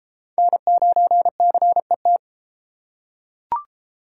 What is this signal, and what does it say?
Morse code, heard as a keyed tone, about 31 wpm, 700 Hz. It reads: N9CET